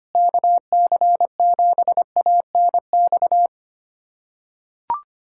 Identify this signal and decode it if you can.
Morse code, heard as a keyed tone, about 25 wpm, 700 Hz. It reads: KC7ANX
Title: USA Callsign DE Callsign 25wpm